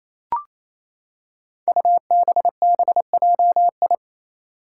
Morse code, heard as a keyed tone, about 28 wpm, 700 Hz. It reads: UBBJI
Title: Five Challenging Letter-Number Combinations 28wpm